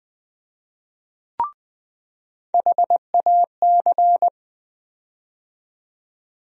Morse code, heard as a keyed tone, about 20 wpm, 700 Hz. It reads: HAC
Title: USA Call Sign Postfixes - 3 Letters 20wpm